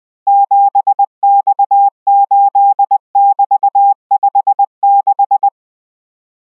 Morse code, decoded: 7X8<BT>56